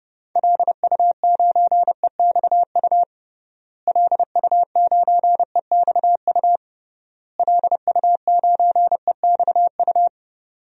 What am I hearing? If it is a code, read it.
Morse code, heard as a keyed tone, about 32 wpm, 700 Hz. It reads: LU9EXU LU9EXU LU9EXU
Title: DXCC-100 Call Signs - 6 Characters - Repeated 3x - 30wpm